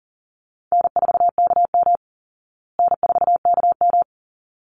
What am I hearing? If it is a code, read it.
Morse code, heard as a keyed tone, about 40 wpm, 700 Hz. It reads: N4KM N4KM
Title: Easy Call Signs Repeated 2x - 40wpm